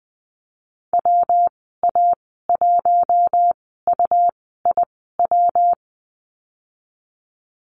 Morse code, heard as a keyed tone, about 20 wpm, spaced as for 14 wpm, 700 Hz. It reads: WA1UIW
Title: 167 Lesson 27g - Number 7 - Call Signs 20wpm